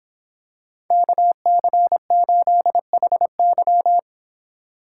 Morse code, heard as a keyed tone, about 26 wpm, 700 Hz. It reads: KC8HY